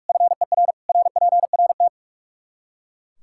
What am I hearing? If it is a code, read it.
Morse code, heard as a keyed tone, about 42 wpm, 700 Hz. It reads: FER RPRT
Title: Practice QSO - Exchange 2 - Info - 45wpm